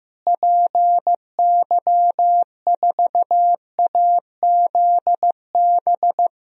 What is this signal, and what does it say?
Morse code, heard as a keyed tone, about 16 wpm, 700 Hz. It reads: PY4AZB
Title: DXCC-100 Call Signs - 6 Characters - Repeated 3x - 15wpm